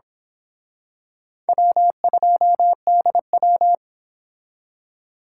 Morse code, heard as a keyed tone, about 26 wpm, 700 Hz. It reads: W2DW
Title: Speed Racing - USA 1x2 Call Signs 22wpm